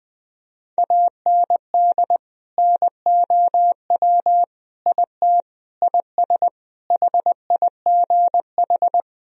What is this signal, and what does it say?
Morse code, heard as a keyed tone, about 20 wpm, 700 Hz. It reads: AND NOW IT IS HIGH
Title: Sentences from Top 1000 Words - Review Part 1 - 20wpm